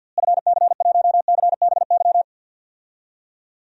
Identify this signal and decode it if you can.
Morse code, heard as a keyed tone, about 50 wpm, 700 Hz. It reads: FC1CBY
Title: DXCC-100 Call Signs - 6 Characters - Repeated 2x - 50wpm